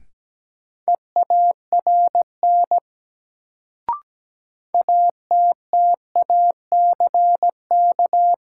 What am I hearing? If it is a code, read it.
Morse code, heard as a keyed tone, about 17 wpm, 700 Hz. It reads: EARN ATTACK
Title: Sentences from Top 1500 Words - New Word Drill - 17wpm